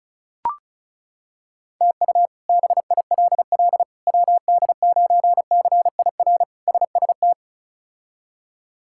Morse code, heard as a keyed tone, about 36 wpm, 700 Hz. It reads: TU BILL WD9CIR SST
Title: K1USN SST QSO Practice - 35wpm